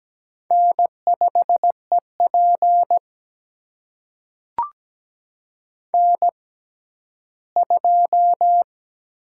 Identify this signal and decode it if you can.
Morse code, heard as a keyed tone, about 17 wpm, spaced as for 12 wpm, 700 Hz. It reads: N5EP N2
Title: USA Call Signs - 4 Characters - 6x Character Spacing - 17wpm